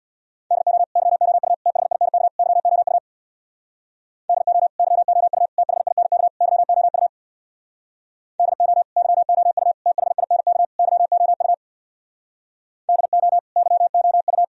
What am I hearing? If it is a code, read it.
Morse code, heard as a keyed tone, about 60 wpm, 700 Hz. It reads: DO YOU THINK YOU DO YOU THINK YOU DO YOU THINK YOU DO YOU
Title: Speed-Racing - Sets of 4 Words 50wpm